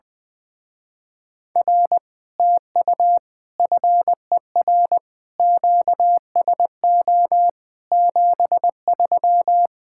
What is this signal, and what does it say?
Morse code, heard as a keyed tone, about 20 wpm, 700 Hz. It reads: R TU FER QSO 73
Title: Speed-Racing Common QSO Phrases Encore - 20wpm